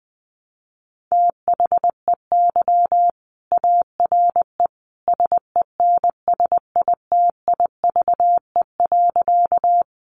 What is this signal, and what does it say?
Morse code, heard as a keyed tone, about 20 wpm, 700 Hz. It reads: THEY ARE SENSITIVE.
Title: Sentences from Top 1400 Words 20wpm